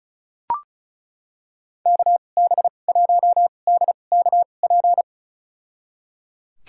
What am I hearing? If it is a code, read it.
Morse code, heard as a keyed tone, about 36 wpm, spaced as for 25 wpm, 700 Hz. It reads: KB1DKP
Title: USA Call Signs - 6 Characters - 2x Character Spacing - 35wpm